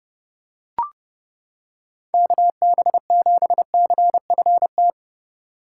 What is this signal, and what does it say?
Morse code, heard as a keyed tone, about 29 wpm, 700 Hz. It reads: KB7CFT